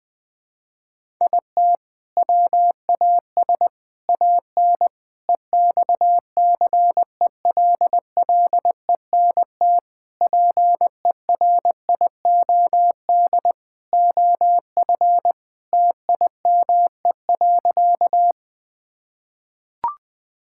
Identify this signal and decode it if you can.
Morse code, heard as a keyed tone, about 21 wpm, 700 Hz. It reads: IT WAS AN EXCELLENT PERIOD OF TIME.
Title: Sentences from Top 1000 Words - Review Part 5 - 20wpm